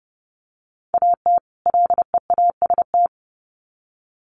Morse code, heard as a keyed tone, about 30 wpm, 700 Hz. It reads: AT LEAST
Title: Sets of 2 Words 30wpm